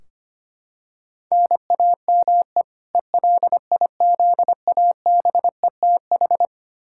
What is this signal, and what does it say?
Morse code, heard as a keyed tone, about 25 wpm, 700 Hz. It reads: NAME ELIZABETH
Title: K1USN SST QSO Practice 25wpm